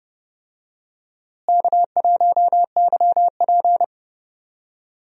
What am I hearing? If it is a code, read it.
Morse code, heard as a keyed tone, about 30 wpm, 700 Hz. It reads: K1YP
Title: USA Call Signs - 4 Characters - Repeated 3x - 30wpm